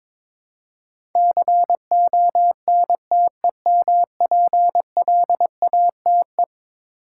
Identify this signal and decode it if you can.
Morse code, heard as a keyed tone, about 22 wpm, 700 Hz. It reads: CONTEMPLATE
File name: practice / 22wpm / Eleven Letter Words - 22wpm